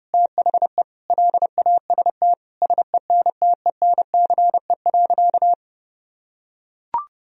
Morse code, heard as a keyed tone, about 30 wpm, 700 Hz. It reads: THE LAST SENTENCE.
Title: Sentences from Top 1000 Words - Review Part 1 - 30wpm